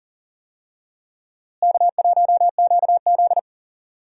Morse code, heard as a keyed tone, about 40 wpm, 700 Hz. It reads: K1QZ
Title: USA Call Signs - 4 Characters - Repeated 3x - 40wpm